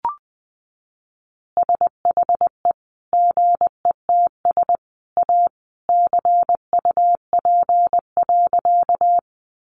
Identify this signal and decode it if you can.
Morse code, heard as a keyed tone, about 20 wpm, 700 Hz. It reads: SHE GETS A CUP.
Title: Additional Easy Phrases 20wpm